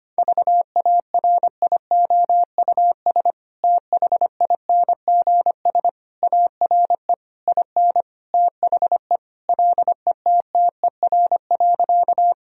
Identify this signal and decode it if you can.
Morse code, heard as a keyed tone, about 25 wpm, 690 Hz. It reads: VARIOUS THINGS ARE IN THE LETTER.
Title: Sentences from Top 1000 Words - Review Part 4 - 25wpm